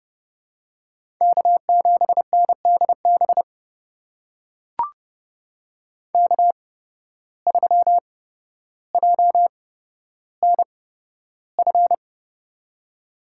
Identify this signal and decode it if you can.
Morse code, heard as a keyed tone, about 31 wpm, spaced as for 13 wpm, 700 Hz. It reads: K7NDB K3JNF
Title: USA Call Signs - 5 Characters - 8x Character Spacing - 30wpm